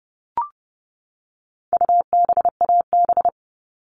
Morse code, heard as a keyed tone, about 30 wpm, 700 Hz. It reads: UBAB
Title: All Cut Numbers - 4-Digit 30wpm